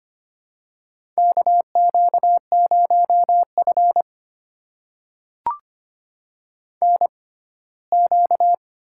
Morse code, heard as a keyed tone, about 25 wpm, spaced as for 17 wpm, 700 Hz. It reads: KQ0F NQ
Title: USA Call Signs - 4 Characters - 6x Character Spacing - 25wpm